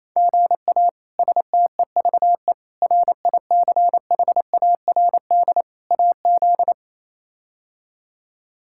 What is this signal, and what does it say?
Morse code, heard as a keyed tone, about 28 wpm, 700 Hz. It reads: GA STEVE RICHARD AZ
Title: K1USN SST QSO Practice - 28wpm